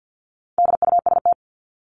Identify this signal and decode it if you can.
Morse code, heard as a keyed tone, about 48 wpm, 700 Hz. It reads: DUST
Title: Top 1751-2000 Words 50wpm